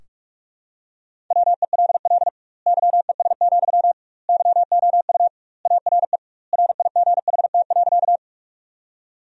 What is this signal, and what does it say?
Morse code, heard as a keyed tone, about 46 wpm, 700 Hz. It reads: WELL YES, YOU ARE RIGHT.
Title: 2x Word-Spacing - Sentences from Top 500 Words - 45wpm